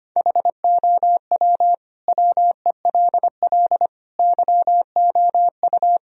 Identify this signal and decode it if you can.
Morse code, heard as a keyed tone, about 25 wpm, 700 Hz. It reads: HOW WELL YOU
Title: Sentences from Top 1000 Words - Review Part 1 - 25wpm